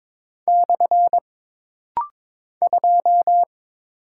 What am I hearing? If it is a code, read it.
Morse code, heard as a keyed tone, about 22 wpm, 700 Hz. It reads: / 2
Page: This rapid-fire timing is geared to push your ICR (Instant Character Recognition) skills.